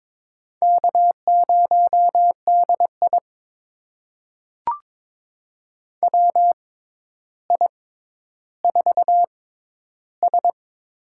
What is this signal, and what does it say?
Morse code, heard as a keyed tone, about 22 wpm, spaced as for 11 wpm, 700 Hz. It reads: K0DI WI4S
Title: USA Call Signs - 4 Characters - 6x Character Spacing - 22wpm